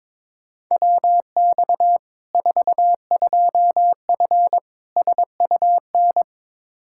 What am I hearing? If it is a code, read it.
Morse code, heard as a keyed tone, about 22 wpm, 700 Hz. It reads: WX 42F SUN
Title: K1USN SST QSO Practice 22wpm